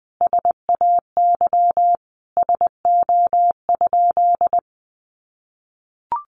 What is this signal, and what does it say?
Morse code, heard as a keyed tone, about 19 wpm, 700 Hz. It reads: SAY SO?
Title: Sentences from Top 1400 Words 20wpm